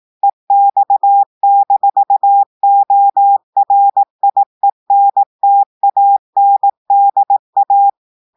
Morse code, heard as a keyed tone, about 18 wpm, 800 Hz. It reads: EX-ORIENTANDA
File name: audios / portugais / portugais texte en clair 1 vitesse 18 mots minutes